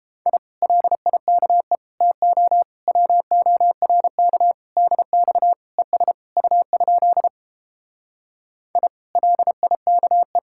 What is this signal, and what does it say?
Morse code, heard as a keyed tone, about 29 wpm, 700 Hz. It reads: I LIKE TO WORK DX ES U? I LIKE